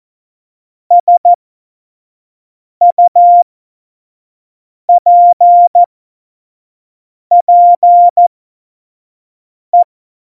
Morse code, heard as a keyed tone, about 14 wpm, spaced as for 5 wpm, 700 Hz. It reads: SUPPE
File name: Morsekurs / Witze / Witz 17.Dezember 2019, 25 F70